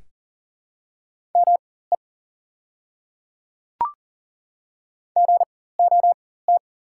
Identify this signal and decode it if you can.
Morse code, heard as a keyed tone, about 37 wpm, spaced as for 18 wpm, 700 Hz. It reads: ME GOT
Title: 4x Character Spacing - Top 300 Words - 40wpm